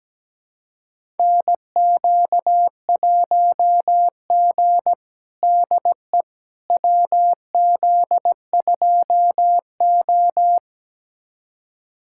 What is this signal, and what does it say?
Morse code, heard as a keyed tone, about 17 wpm, 700 Hz. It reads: NQ1G DE WZ2O
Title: USA Callsign DE Callsign - 4 Characters - 17wpm